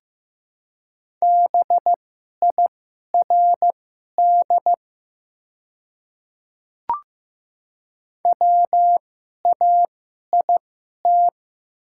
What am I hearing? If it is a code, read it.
Morse code, heard as a keyed tone, about 15 wpm, spaced as for 11 wpm, 700 Hz. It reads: BIRD WAIT
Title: 2x Character Spacing - Top 500 Words - 15wpm